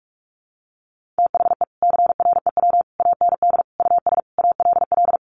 Morse code, heard as a keyed tone, about 44 wpm, 700 Hz. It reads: THE CREW AND US ALL
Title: Sentences from Top 1500 Words 45wpm